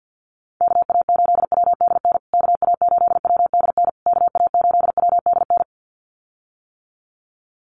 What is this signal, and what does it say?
Morse code, heard as a keyed tone, about 56 wpm, 700 Hz. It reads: KA8WDN KA8WDN KA8WDN
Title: USA Call Signs Repeated 3x 50wpm